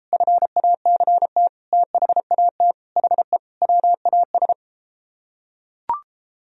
Morse code, heard as a keyed tone, about 33 wpm, 700 Hz. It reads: FACT THAT HE WAS